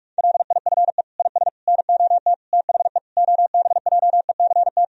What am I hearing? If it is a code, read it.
Morse code, heard as a keyed tone, about 42 wpm, 700 Hz. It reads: LIFE IS NOT THE OBJECT
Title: Sentences from Top 1000 Words - Review Part 3 - 45wpm